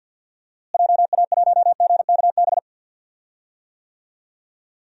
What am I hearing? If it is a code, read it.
Morse code, heard as a keyed tone, about 46 wpm, 700 Hz. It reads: JA1GKB
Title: DXCC-100 Call Signs - 6 Characters - Repeated 2x - 50wpm